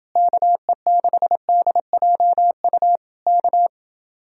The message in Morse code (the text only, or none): KE6DJU K